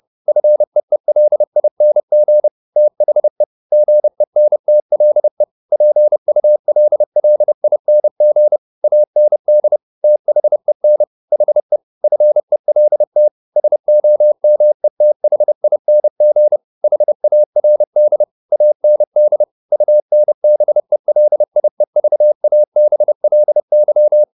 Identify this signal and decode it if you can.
Morse code, heard as a keyed tone, about 30 wpm, 600 Hz. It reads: FEELING THE GENTLE PULLING AND THEN HE FELT SOMETHING HARD AND UNBELIEVABLY
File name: the old man and the sea / the old man and the sea 1 30wpm